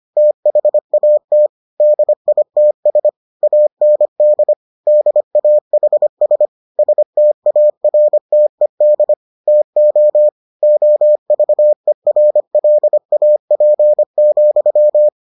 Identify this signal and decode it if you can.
Morse code, heard as a keyed tone, about 25 wpm, 600 Hz. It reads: THAT DITS AND DAHS STARTED TO OVERLAP,